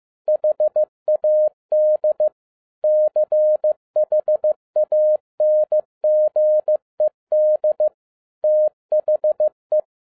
Morse code, heard as a keyed tone, about 15 wpm, 610 Hz. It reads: HAD CHANGED THE